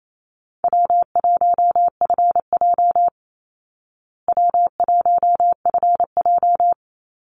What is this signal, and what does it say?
Morse code, heard as a keyed tone, about 28 wpm, 700 Hz. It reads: W1FJ W1FJ
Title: Easy Call Signs Repeated 3x - 28wpm